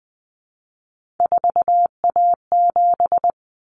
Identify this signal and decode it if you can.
Morse code, heard as a keyed tone, about 20 wpm, 700 Hz. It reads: 4A7
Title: Common Cut Numbers - 3-Digit 20wpm